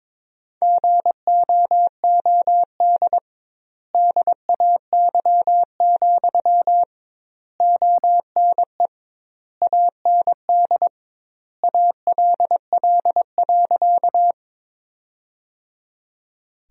Morse code, heard as a keyed tone, about 21 wpm, 700 Hz. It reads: GOOD DAY, ONE AND ALL.
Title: 2x Word-Spacing - Sentences from Top 200 Words - 22wpm